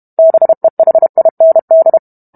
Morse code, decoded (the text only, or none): BEHIND